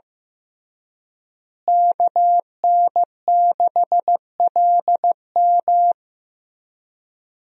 Morse code, heard as a keyed tone, about 15 wpm, 700 Hz. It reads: KN6LM
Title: USA Call Signs - 5 Characters - Repeated 3x - 15wpm